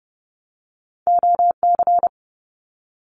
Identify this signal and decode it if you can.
Morse code, heard as a keyed tone, about 30 wpm, 700 Hz. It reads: OC
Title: Abbreviations Repeated 3x - 30wpm